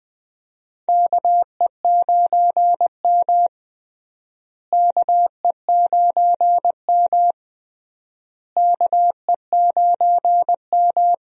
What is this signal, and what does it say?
Morse code, heard as a keyed tone, about 20 wpm, 700 Hz. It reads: KE9M KE9M KE9M
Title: USA Call Signs - 4 Characters - Repeated 3x - 20wpm